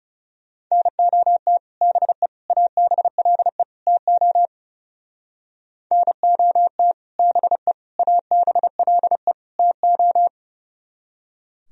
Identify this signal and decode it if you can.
Morse code, heard as a keyed tone, about 37 wpm, 700 Hz. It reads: NOT BE ABLE TO NOT BE ABLE TO
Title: Speed-Racing - Sets of 4 Words 30wpm